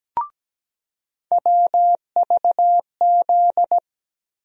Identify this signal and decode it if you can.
Morse code, heard as a keyed tone, about 17 wpm, 700 Hz. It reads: WVZ
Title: USA Call Sign Postfixes - 3 Letters 17wpm